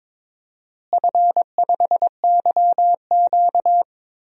Morse code, heard as a keyed tone, about 22 wpm, 700 Hz. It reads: F5YQ
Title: DXCC-100 Call Signs - 4 Characters - Repeated 3x - 22wpm